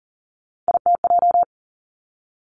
Morse code, heard as a keyed tone, about 42 wpm, 700 Hz. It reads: ITJ
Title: Three Challenging Letter-Number Combinations 40wpm